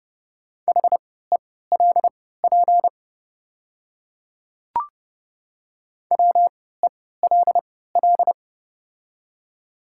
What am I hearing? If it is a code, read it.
Morse code, heard as a keyed tone, about 30 wpm, spaced as for 17 wpm, 690 Hz. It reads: HELP WELL
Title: 3x Character Spacing - Top 200 Words - 30wpm